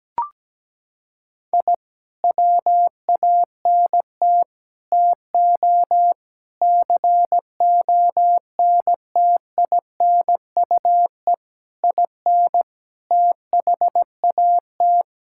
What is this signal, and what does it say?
Morse code, heard as a keyed tone, about 17 wpm, 700 Hz. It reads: I WANT TO CONTINUE IN THAT
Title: Sentences from Top 1000 Words - Review Part 6 - 17wpm